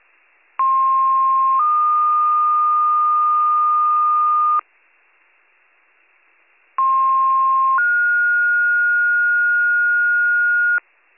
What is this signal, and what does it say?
Two-tone page, heard as a one-second tone, then a three-second tone, about 1060 then 1220 Hz, about 1060 then 1430 Hz, with noise. It is heard from 0.6 to 10.8 s.